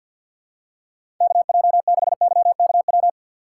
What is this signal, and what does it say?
Morse code, heard as a keyed tone, about 53 wpm, 700 Hz. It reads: K J 6 Y K W
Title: USA Call Signs - 6 Characters - 8x Character Spacing - 50wpm